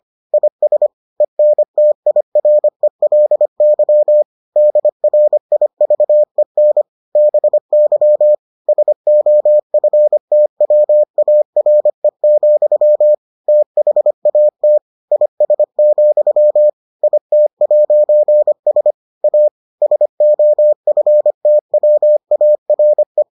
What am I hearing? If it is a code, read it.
Morse code, heard as a keyed tone, about 25 wpm, 600 Hz. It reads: IS ENTIRELY DRIVEN BY SOFTWARE, THAT IS, IT'S A SOFTWARE